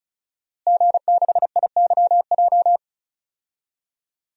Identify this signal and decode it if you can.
Morse code, heard as a keyed tone, about 36 wpm, 700 Hz. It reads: G6IYJ
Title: DXCC-100 Call Signs - 5 Characters - Repeated 3x - 35wpm